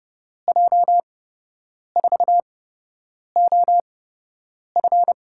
Morse code, heard as a keyed tone, about 32 wpm, spaced as for 8 wpm, 700 Hz. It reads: J4OF
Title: USA Call Signs - 8x Character Spacing - 30wpm